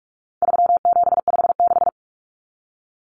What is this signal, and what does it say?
Morse code, heard as a keyed tone, about 44 wpm, 700 Hz. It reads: 3756
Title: Common Cut Numbers - 4-Digit 45wpm